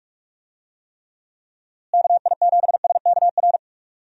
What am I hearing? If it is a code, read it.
Morse code, heard as a keyed tone, about 41 wpm, 700 Hz. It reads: KI7SKR
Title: USA Call Signs - 6 Characters - Repeated 2x - 45wpm